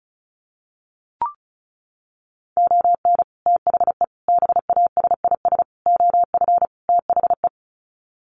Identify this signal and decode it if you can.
Morse code, heard as a keyed tone, about 34 wpm, 700 Hz. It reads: ON THE BASIS OF THE
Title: Sets of 5 Words 35wpm